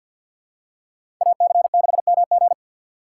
Morse code, heard as a keyed tone, about 53 wpm, 700 Hz. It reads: AC6MG
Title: USA Call Signs - 5 Characters - 2x Character Spacing - 50wpm